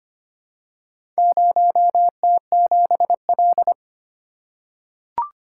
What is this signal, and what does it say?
Morse code, heard as a keyed tone, about 25 wpm, 700 Hz. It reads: 0T7L
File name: practice / 25wpm / Four Letter-Number Combinations - 25wpm